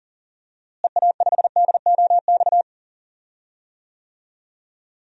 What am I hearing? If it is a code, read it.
Morse code, heard as a keyed tone, about 40 wpm, 700 Hz. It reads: EA5DOX
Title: DXCC-100 Call Signs - 6 Characters - Repeated 2x - 40wpm